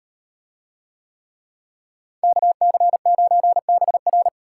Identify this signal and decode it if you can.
Morse code, heard as a keyed tone, about 38 wpm, 700 Hz. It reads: KC9BR